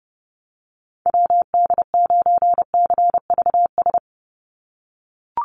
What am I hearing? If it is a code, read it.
Morse code, heard as a keyed tone, about 30 wpm, 700 Hz. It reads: WD9CVS
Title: USA Call Signs 30wpm